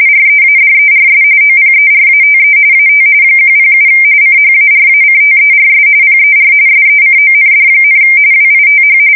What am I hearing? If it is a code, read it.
RTTY, about 45.45 baud. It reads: READERS OPERATE AT 107 CHARACTERS PER SECOND, WITH
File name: fsk170 / teletype-corp